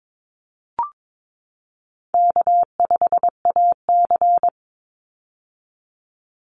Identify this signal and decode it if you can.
Morse code, heard as a keyed tone, about 22 wpm, 700 Hz. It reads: K5AC
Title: USA Call Signs 22wpm